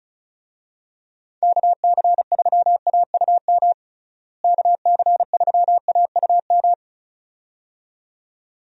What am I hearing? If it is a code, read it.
Morse code, heard as a keyed tone, about 36 wpm, 700 Hz. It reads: KC3AUM KC3AUM
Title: USA Call Signs - 6 Characters - Repeated 2x - 35wpm